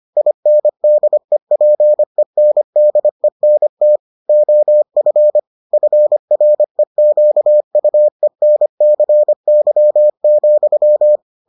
Morse code, heard as a keyed tone, about 25 wpm, 600 Hz. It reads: INDEPENDENT OF FREQUENCY,